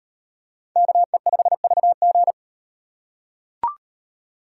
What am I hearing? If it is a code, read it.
Morse code, heard as a keyed tone, about 38 wpm, 700 Hz. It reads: KE5VG